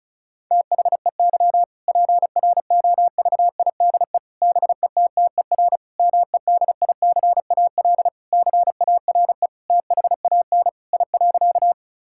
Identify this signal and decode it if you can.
Morse code, heard as a keyed tone, about 36 wpm, 700 Hz. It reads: THEY PROVIDE BETTER MEDICAL CARE THAN I.
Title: Sentences from Top 1000 Words - Review Part 5 - 35wpm